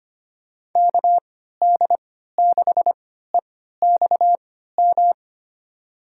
Morse code, heard as a keyed tone, about 25 wpm, spaced as for 14 wpm, 700 Hz. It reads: KD6EXM
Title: USA Call Signs - 6 Characters - 3x Character Spacing - 25wpm